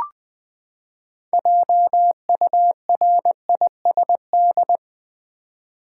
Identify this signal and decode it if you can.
Morse code, heard as a keyed tone, about 20 wpm, 700 Hz. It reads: JURISD
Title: Build A Word - 6 Letters At A Time - 12-Letter Words - 20wpm